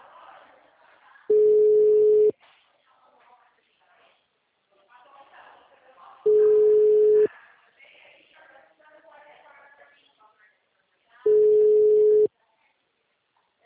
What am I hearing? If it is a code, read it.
Ringback tone (Europe, 425 Hz), with speech.
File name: out- -2028- - - .11999